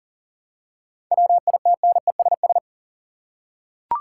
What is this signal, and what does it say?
Morse code, heard as a keyed tone, about 40 wpm, 700 Hz. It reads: WITNESS
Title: Seven Letter Words - 40wpm